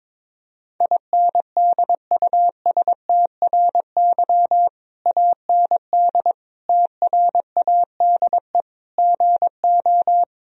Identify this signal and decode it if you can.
Morse code, heard as a keyed tone, about 22 wpm, 700 Hz. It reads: INDUSTRY AND TRADE GO
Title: Sentences from Top 1000 Words - Review Part 4 - 22wpm